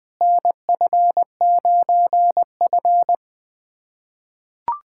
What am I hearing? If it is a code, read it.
Morse code, heard as a keyed tone, about 20 wpm, 700 Hz. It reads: NF9F
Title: USA Callsign DE Callsign - 4 Characters - 20wpm